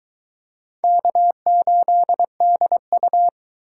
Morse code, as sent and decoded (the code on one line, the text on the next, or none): -.- ---.. -.. ..-
K8DU